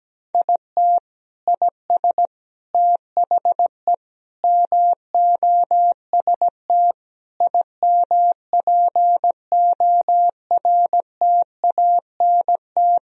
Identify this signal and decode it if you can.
Morse code, heard as a keyed tone, about 17 wpm, 700 Hz. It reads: IT IS THE MOST IMPORTANT
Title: Sentences from Top 1000 Words - Review Part 1 - 17wpm